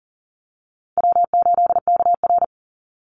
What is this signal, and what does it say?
Morse code, heard as a keyed tone, about 41 wpm, 700 Hz. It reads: W8KR
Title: Easy Call Signs - 40wpm